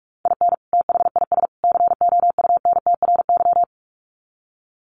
Morse code, heard as a keyed tone, about 44 wpm, 700 Hz. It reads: IN THIS COUNTRY
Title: Sets of 3 Words 45wpm